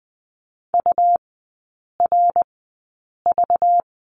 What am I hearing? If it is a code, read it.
Morse code, heard as a keyed tone, about 20 wpm, spaced as for 8 wpm, 690 Hz. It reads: URV
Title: 198 Lesson 32e - Letter k - Three Characters 20wpm